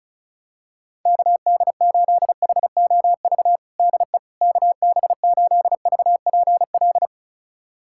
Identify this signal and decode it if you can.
Morse code, heard as a keyed tone, about 36 wpm, 690 Hz. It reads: KD8HOV DE KB8VPL
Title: USA Callsign DE Callsign - 6 Characters - 35wpm